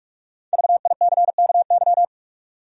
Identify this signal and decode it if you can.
Morse code, heard as a keyed tone, about 46 wpm, 700 Hz. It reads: VICKY
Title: USA Common Names - Max 5 Letters - 45wpm